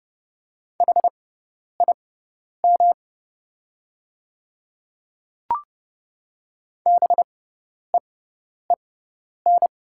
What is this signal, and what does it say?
Morse code, heard as a keyed tone, about 30 wpm, spaced as for 10 wpm, 700 Hz. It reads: HIM BEEN
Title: 6x Character Spacing - Top 200 Words - 30wpm